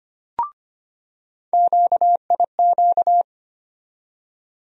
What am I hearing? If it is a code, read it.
Morse code, heard as a keyed tone, about 25 wpm, 700 Hz. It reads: QIQ
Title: USA Call Sign Postfixes - 3 Letters 25wpm